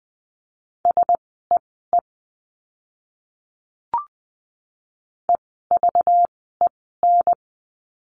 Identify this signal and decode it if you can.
Morse code, heard as a keyed tone, about 20 wpm, spaced as for 14 wpm, 700 Hz. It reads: SEE EVEN
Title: 192 Lesson 31f - Letter v - Words 20wpm